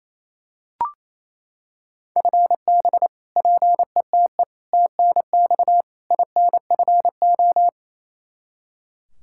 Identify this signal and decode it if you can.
Morse code, heard as a keyed tone, about 28 wpm, 700 Hz. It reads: FB PETE TNX INFO
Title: Practice QSO - Exchange 3 - More Info - 28wpm